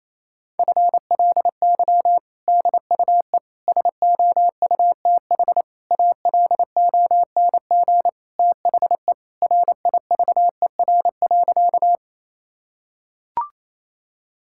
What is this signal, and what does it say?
Morse code, heard as a keyed tone, about 28 wpm, 700 Hz. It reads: FLY DUE SOUTH ALONG THE RIVER.
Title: Sentences from Top 1000 Words - Review Part 6 - 28wpm